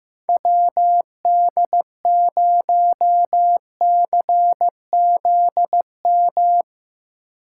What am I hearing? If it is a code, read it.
Morse code, heard as a keyed tone, about 17 wpm, 700 Hz. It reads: WD0CZM
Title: Speed Racing - USA 2x3 Call Signs 15wpm